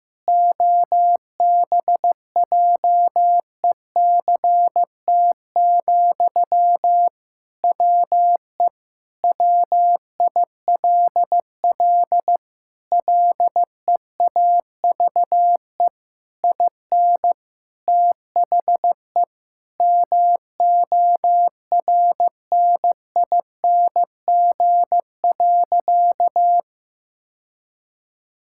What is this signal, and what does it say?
Morse code, heard as a keyed tone, about 15 wpm, 700 Hz. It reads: OBJECT, WE WILL LEAVE IN THE MORNING.
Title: Sentences from Top 1000 Words - Review Part 6 - 15wpm